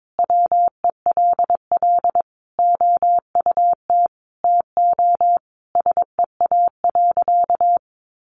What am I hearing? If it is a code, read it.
Morse code, heard as a keyed tone, about 22 wpm, 700 Hz. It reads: WELL OUT TO SEA.
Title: Sentences from Top 400 Words 22wpm